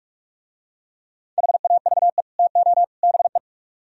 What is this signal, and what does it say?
Morse code, heard as a keyed tone, about 45 wpm, 700 Hz. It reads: HAVE TO BE